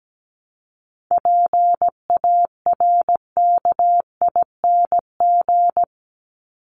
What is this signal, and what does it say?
Morse code, heard as a keyed tone, about 17 wpm, 700 Hz. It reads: PARKING
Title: Top 1501-1750 Words 17wpm